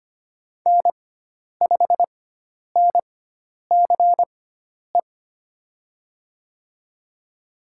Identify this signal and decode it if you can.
Morse code, heard as a keyed tone, about 25 wpm, spaced as for 10 wpm, 700 Hz. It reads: N5NCE
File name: Lesson 16g - Letter c - Call Signs 25wpm